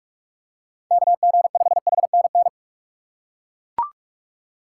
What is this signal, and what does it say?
Morse code, heard as a keyed tone, about 41 wpm, 700 Hz. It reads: KG5HNN
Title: USA Call Signs - 6x Character Spacing - 45wpm